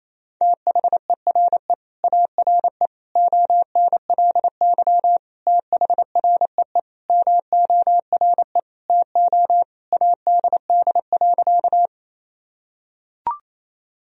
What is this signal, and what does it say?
Morse code, heard as a keyed tone, about 28 wpm, 700 Hz. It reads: THERE ARE ONLY THREE MORE TO ADD.
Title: Sentences from Top 1000 Words - Review Part 1 - 28wpm